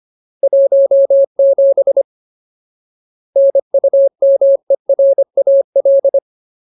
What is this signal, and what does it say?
Morse code, heard as a keyed tone, about 25 wpm, 550 Hz. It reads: 17 NUMERAL
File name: Ultimate Word List / 25WPM Ultimate